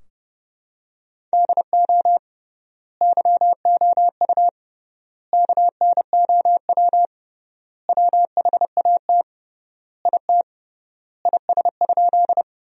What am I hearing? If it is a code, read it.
Morse code, heard as a keyed tone, about 30 wpm, 700 Hz. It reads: DO YOU KNOW WHAT IT IS?
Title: 3x Word-Spacing - Sentences from Top 100 Words - 30wpm